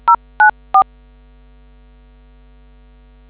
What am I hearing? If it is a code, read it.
DTMF, heard as telephone touch tones, tones of about 100 ms. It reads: 094